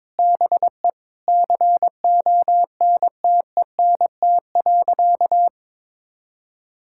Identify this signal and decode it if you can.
Morse code, heard as a keyed tone, about 22 wpm, 700 Hz. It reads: BE CONTENT.
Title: Sentences from Top 1000 Words - Review Part 6 - 22wpm